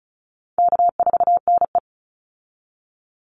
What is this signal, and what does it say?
Morse code, heard as a keyed tone, about 34 wpm, 700 Hz. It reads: K4NE
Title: Easy Call Signs Repeated 3x - 35wpm